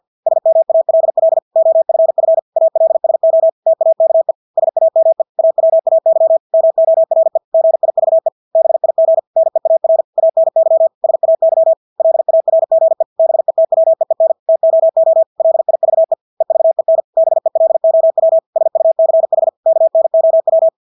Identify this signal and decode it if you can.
Morse code, heard as a keyed tone, about 50 wpm, 650 Hz. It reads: SMALL OFF ALSO TAKE SAME AWAY MORE GIVE BIG NEAR ANY SAY LARGE BETWEEN TOO LIVE EVEN BELOW SUCH KNOW
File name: Engelske ord 50 wpm